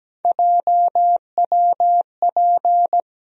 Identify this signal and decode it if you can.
Morse code, heard as a keyed tone, about 17 wpm, 700 Hz. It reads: JWP
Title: USA Call Sign Postfixes - 3 Letters 17wpm